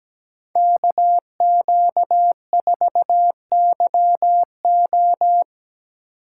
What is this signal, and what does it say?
Morse code, heard as a keyed tone, about 20 wpm, 700 Hz. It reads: KQ4YO
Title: Speed Racing - USA 2x2 Call Signs 17wpm